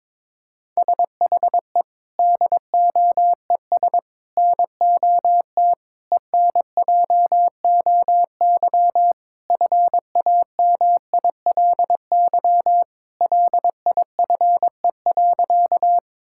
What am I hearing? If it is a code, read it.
Morse code, heard as a keyed tone, about 22 wpm, 700 Hz. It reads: SHE DOES NOT ENJOY FAMILY LIFE.
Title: Sentences from Top 1000 Words - Review Part 5 - 22wpm